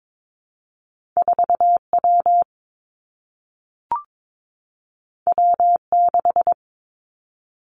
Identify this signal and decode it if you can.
Morse code, heard as a keyed tone, about 22 wpm, 700 Hz. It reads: 4 W W 6
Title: Two Challenging Letter-Number Combinations 22wpm